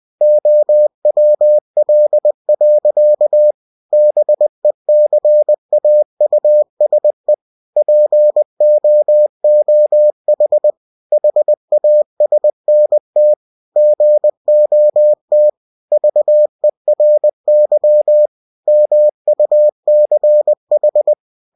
Morse code, heard as a keyed tone, about 20 wpm, 600 Hz. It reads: OWL. BECAUSE POOH HASNT GOT VERY MUCH
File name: cw-books / winnie-the-pooh / winnie-20wpm-page-0097